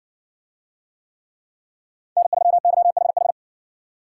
Morse code, heard as a keyed tone, about 55 wpm, 700 Hz. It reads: N3YHH